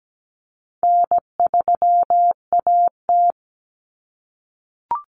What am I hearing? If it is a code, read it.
Morse code, heard as a keyed tone, about 17 wpm, 700 Hz. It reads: N3AT
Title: Common Cut Numbers - 4-Digit 17wpm